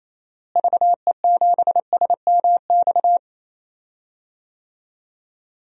Morse code, heard as a keyed tone, about 28 wpm, 700 Hz. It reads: VE7SMX
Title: DXCC-100 Call Signs - 6 Characters - Repeated 2x - 28wpm